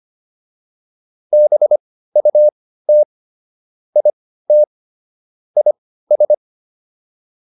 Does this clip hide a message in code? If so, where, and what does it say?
Yes: Morse code from 1.3 to 6.3 s, heard as a keyed tone, about 25 wpm, spaced as for 15 wpm, 600 Hz. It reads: BUT IT IS